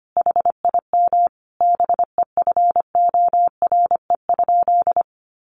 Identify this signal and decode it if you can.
Morse code, heard as a keyed tone, about 25 wpm, 700 Hz. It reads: HIM BEFORE?
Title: Sentences from Top 900 Words 25wpm